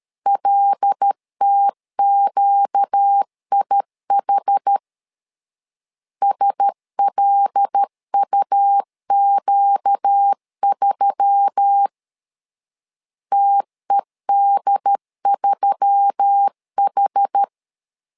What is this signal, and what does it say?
Morse code, heard as a keyed tone, about 13 wpm, 800 Hz. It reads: LTQIH SLUQ3 TED3H
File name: LEZIONI / 5 LBVC / 5 L 12wpm 1913